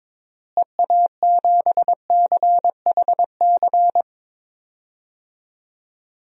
Morse code, heard as a keyed tone, about 22 wpm, 700 Hz. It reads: EA7CHC
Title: DXCC-100 Call Signs - 6 Characters - Repeated 3x - 22wpm